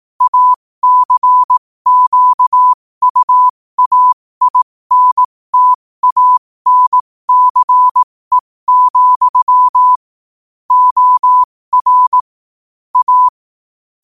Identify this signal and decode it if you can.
Morse code, heard as a keyed tone, about 18 wpm, spaced as for 15 wpm, 1000 Hz. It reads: ACQUAINTANCE, OR A
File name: qotdfiles / 15wpm